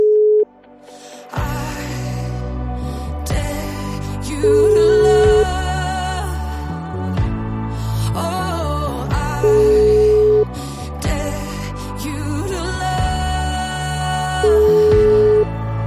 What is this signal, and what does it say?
Ringback tone (Europe, 425 Hz), with noise.